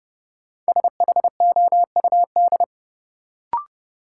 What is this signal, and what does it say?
Morse code, heard as a keyed tone, about 29 wpm, 700 Hz. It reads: SHOUD
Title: Sentences from Top 1000 Words - Review Part 4 - New Word Drill - 30wpm